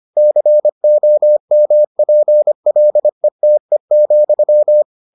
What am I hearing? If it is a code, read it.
Morse code, heard as a keyed tone, about 25 wpm, 600 Hz. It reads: COMPLETE,